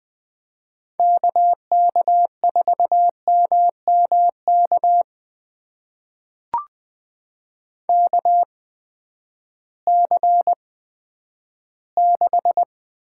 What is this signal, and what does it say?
Morse code, heard as a keyed tone, about 20 wpm, spaced as for 11 wpm, 700 Hz. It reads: KK4MMK KC6
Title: USA Call Signs - 6 Characters - 8x Character Spacing - 20wpm